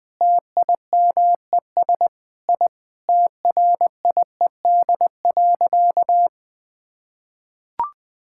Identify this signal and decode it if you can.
Morse code, heard as a keyed tone, about 20 wpm, 700 Hz. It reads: TIMES I TRIED.
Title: Sentences with 8-Word Phrases - 20wpm